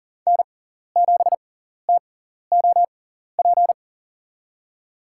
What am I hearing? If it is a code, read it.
Morse code, heard as a keyed tone, about 38 wpm, spaced as for 14 wpm, 700 Hz. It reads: N7TOP
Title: USA Call Signs - 5 Characters - 6x Character Spacing - 40wpm